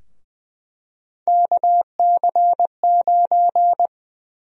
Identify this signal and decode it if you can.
Morse code, heard as a keyed tone, about 21 wpm, 700 Hz. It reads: KC9
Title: USA Call Sign Prefixes - Repeated 2x - 20wpm